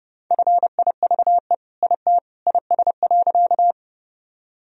Morse code, heard as a keyed tone, about 30 wpm, 700 Hz. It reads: FIVE IT IS.
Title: Sentences from Top 1000 Words - Review Part 2 - 30wpm